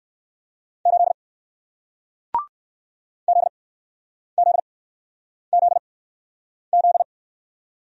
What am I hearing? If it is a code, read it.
Morse code, heard as a keyed tone, about 60 wpm, spaced as for 9 wpm, 700 Hz. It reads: 8 ZZZZ